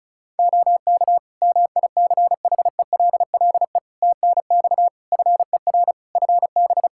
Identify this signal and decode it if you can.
Morse code, heard as a keyed tone, about 34 wpm, 700 Hz. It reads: OK MICHELLE TNX FER FB
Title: Practice QSO - Exchange 4 - Ending - 35wpm